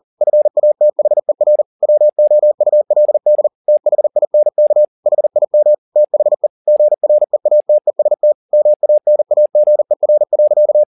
Morse code, heard as a keyed tone, about 40 wpm, 600 Hz. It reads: FATHER WOULD THINK HIM THE GREATEST MANAGER.